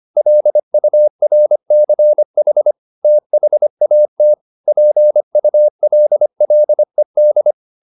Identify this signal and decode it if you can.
Morse code, heard as a keyed tone, about 25 wpm, 600 Hz. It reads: LURCH THAT PULLED
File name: the old man and the sea part 1 25wpm